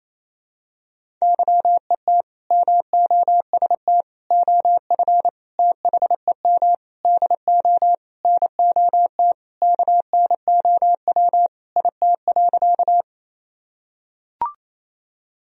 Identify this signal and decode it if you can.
Morse code, heard as a keyed tone, about 28 wpm, 700 Hz. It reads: YET MOST OF THEM DO NOT KNOW IT.
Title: Sentences from Top 1000 Words - Review Part 1 - 28wpm